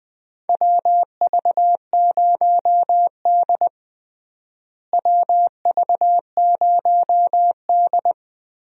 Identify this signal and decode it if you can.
Morse code, heard as a keyed tone, about 20 wpm, 700 Hz. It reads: WV0D WV0D
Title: USA Call Signs - 4 Characters - Repeated 2x - 20wpm